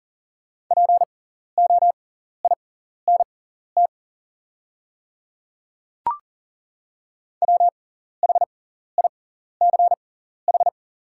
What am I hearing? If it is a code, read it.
Morse code, heard as a keyed tone, about 40 wpm, spaced as for 14 wpm, 690 Hz. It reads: POINT WHICH
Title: 6x Character Spacing - Top 500 Words - 40wpm